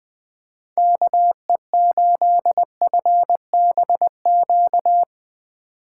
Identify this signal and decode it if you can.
Morse code, heard as a keyed tone, about 20 wpm, 700 Hz. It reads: KE8FBQ
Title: USA Call Signs - 6 Characters - 4x Character Spacing - 20wpm